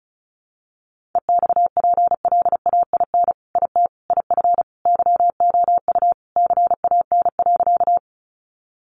Morse code, heard as a keyed tone, about 34 wpm, 700 Hz. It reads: EXPLAIN IT IF YOU CAN.
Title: Sentences from Top 1100 Words 35wpm